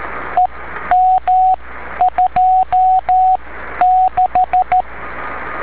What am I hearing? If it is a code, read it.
Morse code, heard as a keyed tone, about 13 wpm, spaced as for 10 wpm, 740 Hz, with radio noise. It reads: EM26
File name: beacons / w5gpm-b